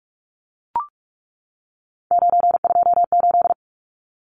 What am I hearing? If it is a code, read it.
Morse code, heard as a keyed tone, about 44 wpm, 700 Hz. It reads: 928
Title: Numbers - 3-Digit 45wpm